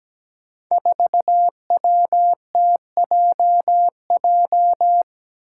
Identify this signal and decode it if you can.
Morse code, heard as a keyed tone, about 17 wpm, 700 Hz. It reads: 4WTJJ
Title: Five Challenging Letter-Number Combinations 17wpm